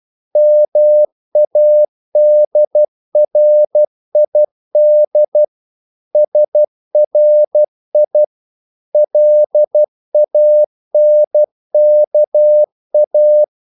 Morse code, heard as a keyed tone, about 12 wpm, 600 Hz. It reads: MADRID SRI LANKA
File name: CW / Practice / NationalCapitals12